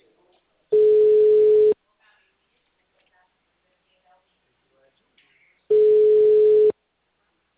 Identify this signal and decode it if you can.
Ringback tone (Europe, 425 Hz), with speech.